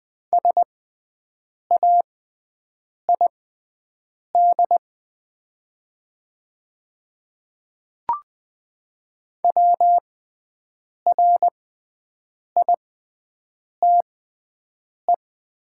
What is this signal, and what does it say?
Morse code, heard as a keyed tone, about 20 wpm, spaced as for 7 wpm, 700 Hz. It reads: SAID WRITE
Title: 6x Character Spacing - Top 100 Words - 20wpm